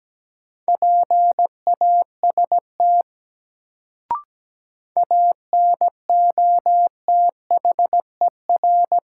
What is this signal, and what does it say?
Morse code, heard as a keyed tone, about 17 wpm, 700 Hz. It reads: PAST ANOTHER
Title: Sentences from Top 600 Words - New Word Drill - 17wpm